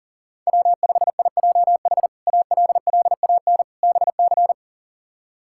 Morse code, heard as a keyed tone, about 37 wpm, 690 Hz. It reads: W5IJH ALLAN BC
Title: K1USN SST QSO Practice - 40wpm